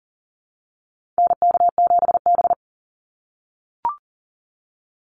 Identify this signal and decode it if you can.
Morse code, heard as a keyed tone, about 40 wpm, 700 Hz. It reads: NK7B
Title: Call Signs - 40wpm